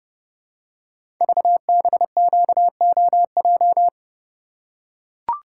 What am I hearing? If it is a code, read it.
Morse code, heard as a keyed tone, about 30 wpm, 700 Hz. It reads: VBQOJ
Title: Code Groups - 5 Letters - 30wpm